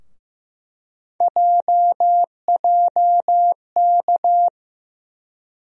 Morse code, heard as a keyed tone, about 15 wpm, 700 Hz. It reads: JJK
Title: USA Call Sign Postfixes - 3 Letters 15wpm